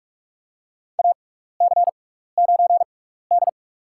Morse code, heard as a keyed tone, about 46 wpm, spaced as for 16 wpm, 700 Hz. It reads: AC9D
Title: USA Call Signs - 4 Characters - 6x Character Spacing - 45wpm